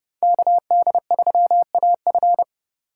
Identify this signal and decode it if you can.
Morse code, heard as a keyed tone, about 29 wpm, 700 Hz. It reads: KD3AF